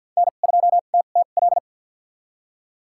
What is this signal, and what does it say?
Morse code, heard as a keyed tone, about 52 wpm, spaced as for 36 wpm, 700 Hz. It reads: N2TTL
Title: USA Call Signs - 5 Characters - 2x Character Spacing - 50wpm